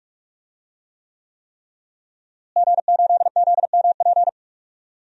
Morse code, heard as a keyed tone, about 46 wpm, 700 Hz. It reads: G8ZMP
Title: DXCC-100 Call Signs - 5 Characters - Repeated 3x - 45wpm